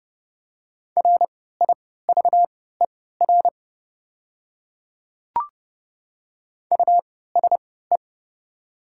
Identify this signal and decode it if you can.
Morse code, heard as a keyed tone, about 30 wpm, spaced as for 17 wpm, 700 Hz. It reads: RIVER USE
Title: 3x Character Spacing - Top 300 Words - 30wpm